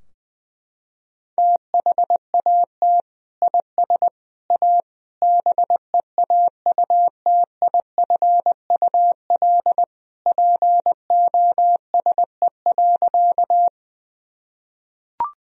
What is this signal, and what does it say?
Morse code, heard as a keyed tone, about 21 wpm, 700 Hz. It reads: THAT IS A BEAUTIFUL POSE.
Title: Sentences from Top 1000 Words - Review Part 2 - 20wpm